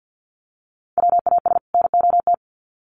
Morse code, heard as a keyed tone, about 50 wpm, 700 Hz. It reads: WAS NOT
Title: Sets of 2 Words 50wpm